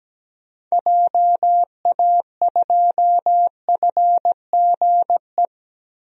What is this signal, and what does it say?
Morse code, heard as a keyed tone, about 17 wpm, 700 Hz. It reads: JA2FGE
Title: DXCC-100 Call Signs - 6 Characters - Repeated 2x - 17wpm